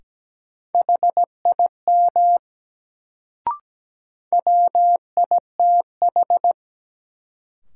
Morse code, heard as a keyed tone, about 17 wpm, 700 Hz. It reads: HIM WITH
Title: Sets of 2 Words - New Word Drill - 17wpm